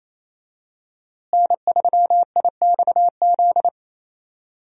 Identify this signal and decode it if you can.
Morse code, heard as a keyed tone, about 28 wpm, 700 Hz. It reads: N3IXZ
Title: USA Call Signs - 5 Characters - Repeated 2x - 28wpm